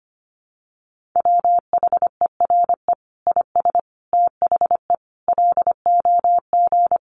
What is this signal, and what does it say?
Morse code, heard as a keyed tone, about 25 wpm, 700 Hz. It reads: WHERE IS THE LOG
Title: Sentences from Top 1400 Words 25wpm